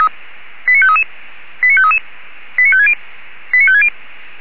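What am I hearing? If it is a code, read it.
Five-tone selective call (ZVEI), heard as quick runs of five tones, tones of about 70 ms. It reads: 89533 89533 89560 89560
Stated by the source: ZVEI five-tone selective call received on SDR.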